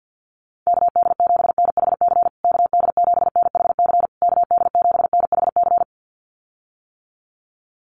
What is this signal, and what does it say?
Morse code, heard as a keyed tone, about 50 wpm, 700 Hz. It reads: KD7NHC KD7NHC KD7NHC
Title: USA Call Signs Repeated 3x 50wpm